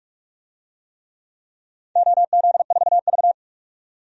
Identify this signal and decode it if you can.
Morse code, heard as a keyed tone, about 47 wpm, 700 Hz. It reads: OZ4V
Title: DXCC-100 Call Signs - 4 Characters - 45wpm